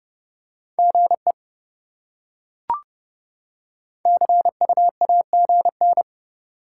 Morse code, heard as a keyed tone, about 28 wpm, 700 Hz. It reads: GE CUAGN